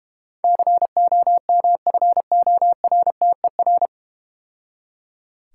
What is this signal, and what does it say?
Morse code, heard as a keyed tone, about 31 wpm, 700 Hz. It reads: COMFORTER
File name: Easy Words - Speed-Racing with Hints - 25wpm